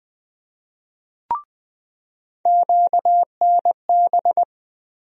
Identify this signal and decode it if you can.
Morse code, heard as a keyed tone, about 20 wpm, 700 Hz. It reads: QNB
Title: USA Call Sign Postfixes - 3 Letters 20wpm